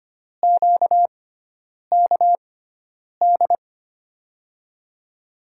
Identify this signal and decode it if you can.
Morse code, heard as a keyed tone, about 25 wpm, spaced as for 9 wpm, 700 Hz. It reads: QKD
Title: USA Call Signs - 6 Characters - 6x Character Spacing - 25wpm